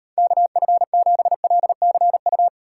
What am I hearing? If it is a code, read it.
Morse code, heard as a keyed tone, about 31 wpm, 700 Hz. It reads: KF7LCU
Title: Speed Racing - USA Call Signs 25wpm